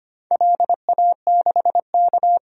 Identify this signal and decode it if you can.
Morse code, heard as a keyed tone, about 25 wpm, 700 Hz. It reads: LA6K
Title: DXCC-100 Call Signs - 4 Characters - 25wpm